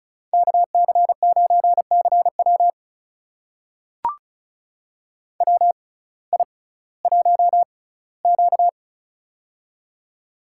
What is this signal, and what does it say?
Morse code, heard as a keyed tone, about 36 wpm, spaced as for 19 wpm, 700 Hz. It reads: KC9CW WI1Q
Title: USA Call Signs - 6x Character Spacing - 35wpm